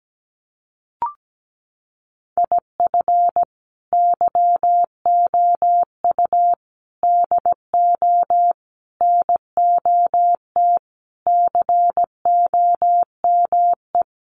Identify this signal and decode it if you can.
Morse code, heard as a keyed tone, about 17 wpm, 700 Hz. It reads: IF YOU DO NOT COME
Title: Sentences from Top 800 Words 17wpm